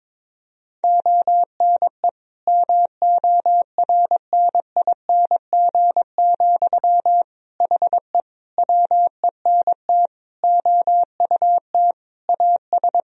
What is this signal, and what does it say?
Morse code, heard as a keyed tone, about 22 wpm, 700 Hz. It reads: ONE MORNING, HE WENT OUT AS
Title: Sentences from Top 1000 Words - Review Part 2 - 22wpm